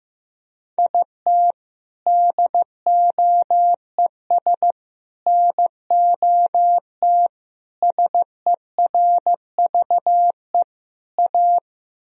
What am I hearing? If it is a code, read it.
Morse code, heard as a keyed tone, about 15 wpm, 700 Hz. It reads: IT DOES NOT SERVE A
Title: Sentences from Top 1000 Words - Review Part 2 - 15wpm